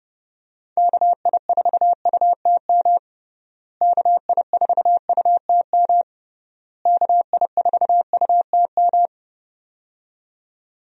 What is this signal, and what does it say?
Morse code, heard as a keyed tone, about 31 wpm, 700 Hz. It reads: KI4UTM KI4UTM KI4UTM
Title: USA Call Signs - 6 Characters - Repeated 3x - 30wpm